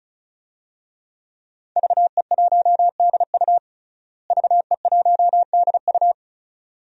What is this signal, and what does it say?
Morse code, heard as a keyed tone, about 36 wpm, 700 Hz. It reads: VE1DU VE1DU
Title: DXCC-100 Call Signs - 5 Characters - Repeated 2x - 35wpm